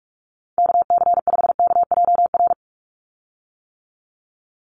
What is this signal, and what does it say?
Morse code, heard as a keyed tone, about 44 wpm, 700 Hz. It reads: KC5KJR
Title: USA Call Signs 45wpm